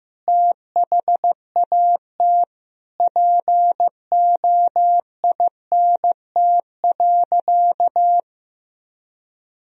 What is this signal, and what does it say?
Morse code, heard as a keyed tone, about 15 wpm, 700 Hz. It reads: THAT POINT.
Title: Sentences from Top 1000 Words - Review Part 1 - 15wpm